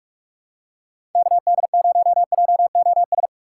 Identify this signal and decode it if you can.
Morse code, heard as a keyed tone, about 44 wpm, 700 Hz. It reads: KD0JOS
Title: USA Call Signs - 6 Characters - 6x Character Spacing - 45wpm